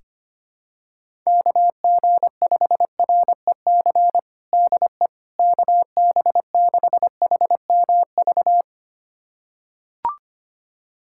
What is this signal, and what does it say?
Morse code, heard as a keyed tone, about 25 wpm, 700 Hz. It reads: KG5REC DE KB6HMV
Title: USA Callsign DE Callsign - 6 Characters - 25wpm